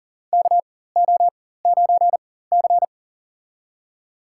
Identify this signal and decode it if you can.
Morse code, heard as a keyed tone, about 37 wpm, spaced as for 18 wpm, 700 Hz. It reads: KO9C
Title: USA Call Signs - 4 Characters - 4x Character Spacing - 40wpm